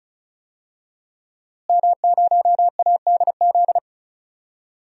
Morse code, heard as a keyed tone, about 34 wpm, 700 Hz. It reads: M0ADZ
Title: DXCC-100 Call Signs - 5 Characters - Repeated 3x - 35wpm